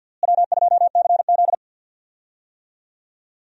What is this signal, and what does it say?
Morse code, heard as a keyed tone, about 49 wpm, 700 Hz. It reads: W2CZ
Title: USA Call Signs - 4 Characters - Repeated 2x - 50wpm